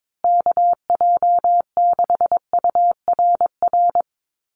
Morse code, heard as a keyed tone, about 22 wpm, 700 Hz. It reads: KJ6URR
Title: USA Call Signs Repeated 2x 22wpm